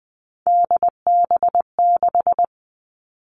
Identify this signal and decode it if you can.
Morse code, heard as a keyed tone, about 20 wpm, 700 Hz. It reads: DB6
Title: All Cut Numbers - 3-Digit 20wpm